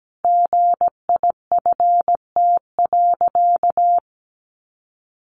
Morse code, heard as a keyed tone, about 17 wpm, 700 Hz. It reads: GIFT.
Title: Sentences from Top 1400 Words 17wpm